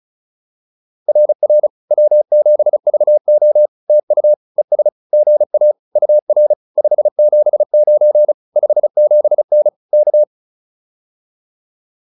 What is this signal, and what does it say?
Morse code, heard as a keyed tone, about 34 wpm, 600 Hz. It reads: RR W7VO TU ES GA UR 579 57N K